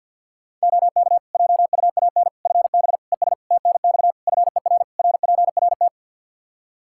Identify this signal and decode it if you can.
Morse code, heard as a keyed tone, about 50 wpm, 700 Hz. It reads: OK JUAN FB ES TNX FER RPRT
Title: K1USN SST QSO Practice 50wpm